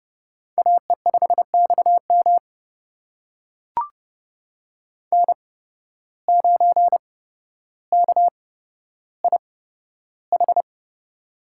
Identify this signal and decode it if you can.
Morse code, heard as a keyed tone, about 32 wpm, spaced as for 13 wpm, 700 Hz. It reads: AE5XM N9KIH
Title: USA Call Signs - 5 Characters - 8x Character Spacing - 30wpm